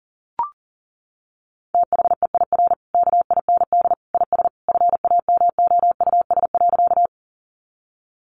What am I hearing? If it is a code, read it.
Morse code, heard as a keyed tone, about 40 wpm, 700 Hz. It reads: THEIR KIND IS FAMOUS.
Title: Sentences from Top 1500 Words 40wpm